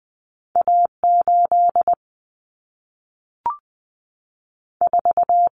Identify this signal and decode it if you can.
Morse code, heard as a keyed tone, about 20 wpm, 700 Hz. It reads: A8 4
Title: Common Cut Numbers - 2-Digit 20wpm